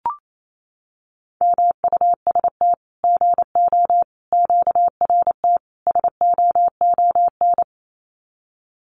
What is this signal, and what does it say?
Morse code, heard as a keyed tone, about 28 wpm, 700 Hz. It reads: MUST GO QRT SOON
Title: Common QSO Phrases 28wpm